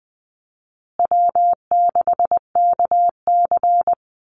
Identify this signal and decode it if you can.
Morse code, heard as a keyed tone, about 20 wpm, 700 Hz. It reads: W6KC
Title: Easy Call Signs Repeated 3x - 20wpm